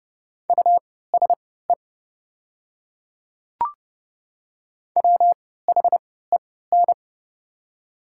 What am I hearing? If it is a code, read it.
Morse code, heard as a keyed tone, about 29 wpm, spaced as for 17 wpm, 690 Hz. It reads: USE WHEN
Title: 3x Character Spacing - Top 200 Words - 30wpm